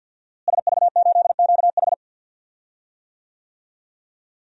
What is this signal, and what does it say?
Morse code, heard as a keyed tone, about 51 wpm, 700 Hz. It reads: SV8QH
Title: DXCC-100 Call Signs - 5 Characters - Repeated 2x - 50wpm